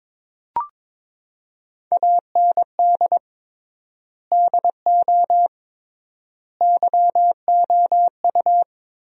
Morse code, heard as a keyed tone, about 22 wpm, 700 Hz. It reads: AND DO YOU
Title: 3x Word-Spacing - Sentences from Top 100 Words - 22wpm